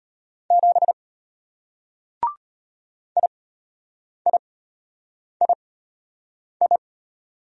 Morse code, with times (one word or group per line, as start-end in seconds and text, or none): 0.5-0.9 7
3.2-6.8 IIII